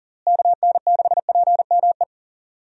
Morse code, heard as a keyed tone, about 37 wpm, 700 Hz. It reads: KN6PME
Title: USA Call Signs - 6 Characters - Repeated 2x - 40wpm